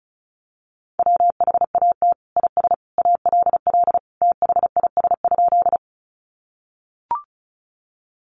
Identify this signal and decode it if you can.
Morse code, heard as a keyed tone, about 34 wpm, 700 Hz. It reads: WHAT IS ALL THIS?
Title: Sentences from Top 100 Words 35wpm